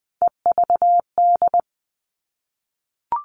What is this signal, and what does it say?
Morse code, heard as a keyed tone, about 20 wpm, 700 Hz. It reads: EVD
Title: All Cut Numbers - 3-Digit 20wpm